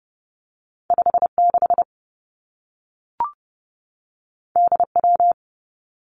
Morse code, heard as a keyed tone, about 30 wpm, 700 Hz. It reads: 5 6 D W
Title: Two Challenging Letter-Number Combinations 30wpm